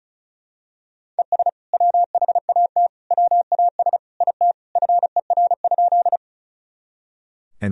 Morse code, heard as a keyed tone, about 36 wpm, 700 Hz. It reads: ES WHAT WAS IT FER?
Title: Sentences from Top 100 Words and Abbreviations 35wpm